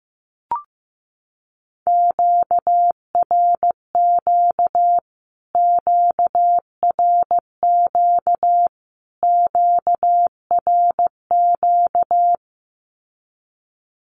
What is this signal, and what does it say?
Morse code, heard as a keyed tone, about 15 wpm, 700 Hz. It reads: QRQ QRQ QRQ
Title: Q-Codes Repeated 3x 15wpm